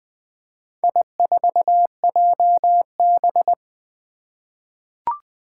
Morse code, heard as a keyed tone, about 20 wpm, 700 Hz. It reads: I4JB
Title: Four Challenging Letter-Number Combinations 20wpm